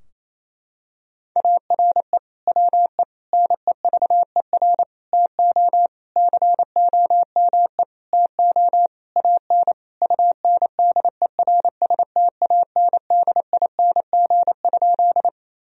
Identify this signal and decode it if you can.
Morse code, heard as a keyed tone, about 28 wpm, 700 Hz. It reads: ARE WE NEVER TO COME TO AN UNDERSTANDING?
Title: Sentences from Top 1000 Words - Review Part 3 - 28wpm